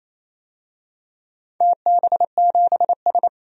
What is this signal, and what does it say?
Morse code, heard as a keyed tone, about 29 wpm, 700 Hz. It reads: TB7S
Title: DXCC-100 Call Signs - 4 Characters - 28wpm